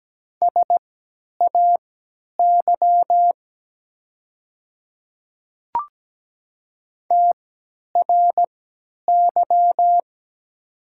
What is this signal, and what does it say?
Morse code, heard as a keyed tone, about 17 wpm, spaced as for 10 wpm, 690 Hz. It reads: SAY TRY
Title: 3x Character Spacing - Top 200 Words - 17wpm